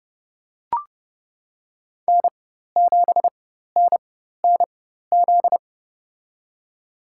Morse code, heard as a keyed tone, about 30 wpm, spaced as for 14 wpm, 700 Hz. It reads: N7NNZ
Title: USA Call Signs - 4x Character Spacing - 30wpm